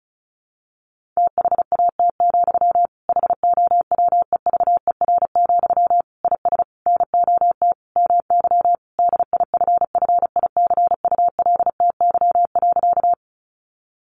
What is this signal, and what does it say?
Morse code, heard as a keyed tone, about 34 wpm, 700 Hz. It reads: THAT, HOWEVER, IS NOT MY DIFFICULTY.
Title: Sentences from Top 1500 Words 35wpm